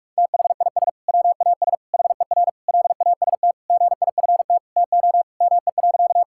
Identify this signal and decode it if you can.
Morse code, heard as a keyed tone, about 46 wpm, 700 Hz. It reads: THIS WAS HER LAST GIFT TO ME.
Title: Sentences from Top 1000 Words - Review Part 6 - 45wpm